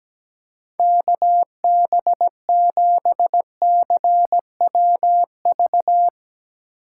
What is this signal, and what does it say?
Morse code, heard as a keyed tone, about 17 wpm, 700 Hz. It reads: KB7CWV
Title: USA Call Signs - 6 Characters - 3x Character Spacing - 17wpm